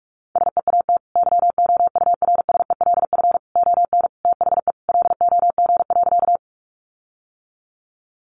Morse code, heard as a keyed tone, about 44 wpm, 700 Hz. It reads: SEAT YOURSELF ON THE LOG.
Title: Sentences from Top 1500 Words 45wpm